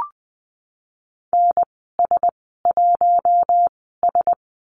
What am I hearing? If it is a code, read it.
Morse code, heard as a keyed tone, about 20 wpm, spaced as for 15 wpm, 700 Hz. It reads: NS1S
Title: 042 Lesson 8g - Number 1 - Call Signs 20wpm